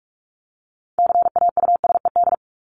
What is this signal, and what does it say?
Morse code, heard as a keyed tone, about 44 wpm, 700 Hz. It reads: CAUSED
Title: ED Suffix - Top 100 Words with Base Words 45wpm